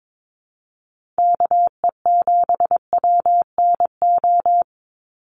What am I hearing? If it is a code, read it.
Morse code, heard as a keyed tone, about 22 wpm, 700 Hz. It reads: KE7WNO
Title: USA Call Signs 22wpm